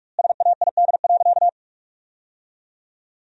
Morse code, heard as a keyed tone, about 44 wpm, 700 Hz. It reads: SAID.
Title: 2x Word-Spacing - Sentences from Top 400 Words - 45wpm